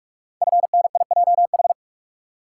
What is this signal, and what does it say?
Morse code, heard as a keyed tone, about 44 wpm, 690 Hz. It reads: FNIJH
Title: Code Groups - 5 Letters - 45wpm